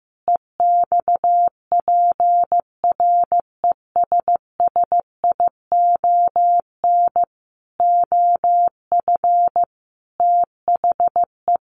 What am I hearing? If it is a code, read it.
Morse code, heard as a keyed tone, about 15 wpm, 700 Hz. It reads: EXPRESSION OF THE
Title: Sentences from Top 1500 Words 15wpm